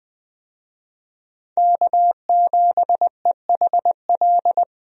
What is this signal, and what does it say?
Morse code, heard as a keyed tone, about 20 wpm, 700 Hz. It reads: K7EHL
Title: USA Call Signs - 5 Characters - 20wpm